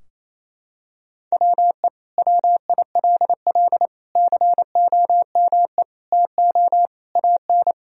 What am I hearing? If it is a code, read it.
Morse code, heard as a keyed tone, about 27 wpm, 690 Hz. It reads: WE WILL COME TO AN
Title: Sentences from Top 1000 Words - Review Part 3 - 28wpm